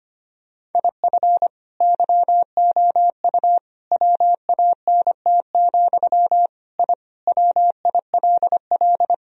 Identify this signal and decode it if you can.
Morse code, heard as a keyed tone, about 25 wpm, 700 Hz. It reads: IF YOU WANT, I WILL
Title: Sentences from Top 1000 Words - Review Part 3 - 25wpm